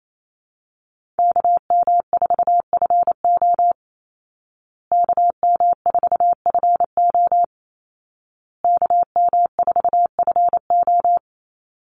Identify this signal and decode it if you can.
Morse code, heard as a keyed tone, about 28 wpm, 700 Hz. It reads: KM4FO KM4FO KM4FO
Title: Call Signs Repeated 2x - 28wpm